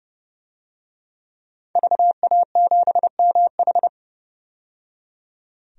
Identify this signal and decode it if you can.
Morse code, heard as a keyed tone, about 30 wpm, 700 Hz. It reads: VA7MH
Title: DXCC-100 Call Signs - 5 Characters - 30wpm